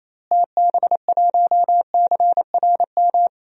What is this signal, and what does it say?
Morse code, heard as a keyed tone, about 27 wpm, 700 Hz. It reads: TB1CRM
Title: DXCC-100 Call Signs - 6 Characters - Repeated 3x - 28wpm